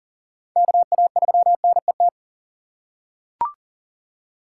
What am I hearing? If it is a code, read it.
Morse code, heard as a keyed tone, about 37 wpm, 700 Hz. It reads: KA3NET
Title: USA Call Signs - 6 Characters - 4x Character Spacing - 40wpm